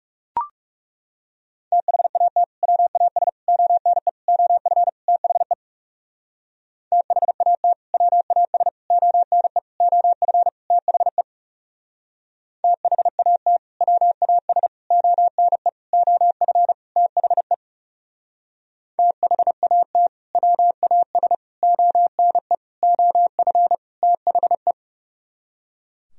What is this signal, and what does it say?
Morse code, heard as a keyed tone, about 37 wpm, 700 Hz. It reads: THAT WAS ONE OF THE THAT WAS ONE OF THE THAT WAS ONE OF THE THAT WAS ONE OF THE
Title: Speed-Racing - Sets of 5 Words 30wpm